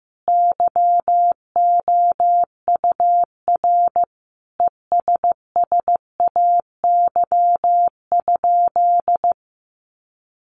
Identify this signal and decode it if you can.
Morse code, heard as a keyed tone, about 15 wpm, 700 Hz. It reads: YOUR ESSAY?
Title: Sentences from Top 1500 Words 15wpm